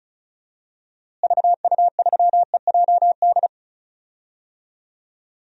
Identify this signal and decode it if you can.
Morse code, heard as a keyed tone, about 34 wpm, 700 Hz. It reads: VU3EJD
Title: DXCC-100 Call Signs - 6 Characters - Repeated 3x - 35wpm